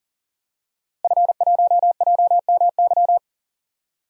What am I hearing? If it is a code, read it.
Morse code, heard as a keyed tone, about 38 wpm, 700 Hz. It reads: F1JMY
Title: DXCC-100 Call Signs - 5 Characters - Repeated 2x - 40wpm